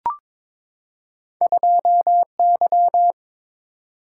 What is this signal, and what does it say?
Morse code, heard as a keyed tone, about 22 wpm, 700 Hz. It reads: 2Y
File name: practice / 22wpm / Two Letter-Number Combinations - 22wpm